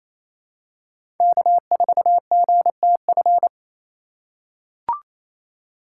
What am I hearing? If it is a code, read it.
Morse code, heard as a keyed tone, about 28 wpm, 700 Hz. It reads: K4GTF
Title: USA Call Signs - 5 Characters - 2x Character Spacing - 28wpm